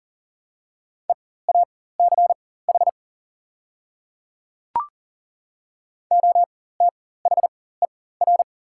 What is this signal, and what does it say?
Morse code, heard as a keyed tone, about 40 wpm, spaced as for 19 wpm, 700 Hz. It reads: EACH OTHER
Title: 4x Character Spacing - Top 100 Words - 40wpm